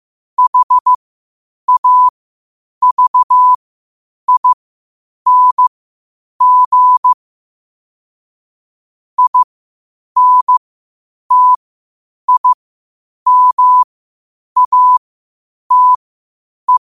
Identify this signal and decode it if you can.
Morse code, heard as a keyed tone, about 15 wpm, spaced as for 8 wpm, 1000 Hz. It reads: HAVING INTIMATE
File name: 8wpm